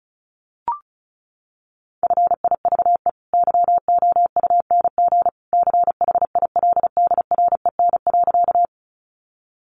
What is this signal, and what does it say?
Morse code, heard as a keyed tone, about 34 wpm, 700 Hz. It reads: FIVE YOUNG CHILDREN.
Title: Sentences from Top 400 Words 35wpm